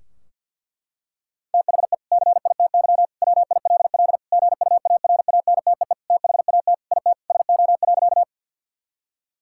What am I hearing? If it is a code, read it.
Morse code, heard as a keyed tone, about 50 wpm, 700 Hz. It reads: THE CITY WILL GUARANTEE THAT IT SO.
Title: Sentences from Top 1600 Words 50wpm